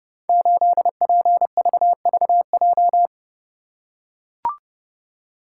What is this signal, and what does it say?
Morse code, heard as a keyed tone, about 30 wpm, 700 Hz. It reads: 8PVVJ
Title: Code Groups - 5 Letter:Numbers - 30wpm